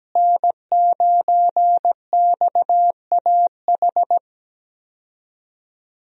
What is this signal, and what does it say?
Morse code, heard as a keyed tone, about 17 wpm, 700 Hz. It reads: N9XAH
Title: USA Call Signs - 5 Characters - Repeated 2x - 17wpm